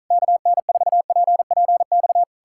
Morse code, heard as a keyed tone, about 46 wpm, 700 Hz. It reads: KN4PPX